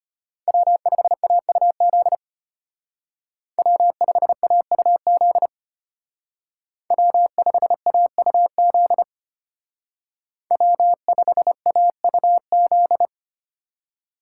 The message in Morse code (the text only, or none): W5AUZ W5AUZ W5AUZ W5AUZ